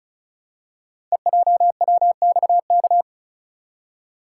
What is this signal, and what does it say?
Morse code, heard as a keyed tone, about 36 wpm, 690 Hz. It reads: EJWXK
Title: Code Groups - 5 Letters - 35wpm